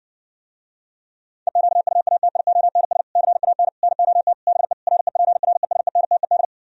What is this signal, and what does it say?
Morse code, heard as a keyed tone, about 60 wpm, 700 Hz. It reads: EQUATIONS CAN NOT BE REPRESENTED